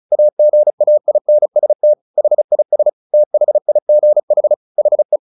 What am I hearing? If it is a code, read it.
Morse code, heard as a keyed tone, about 34 wpm, 600 Hz. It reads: AGAINST HIS THIGH HE
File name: the old man and the sea / the old man and the sea 1 35wpm